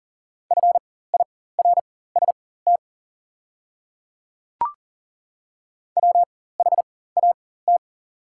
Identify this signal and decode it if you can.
Morse code, heard as a keyed tone, about 40 wpm, spaced as for 19 wpm, 690 Hz. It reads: FIRST WHAT
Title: 4x Character Spacing - Top 100 Words - 40wpm